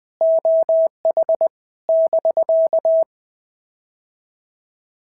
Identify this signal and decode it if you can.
Morse code, heard as a keyed tone, about 20 wpm, 650 Hz. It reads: OH <BK>
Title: POTA QSO Practice - 20wpm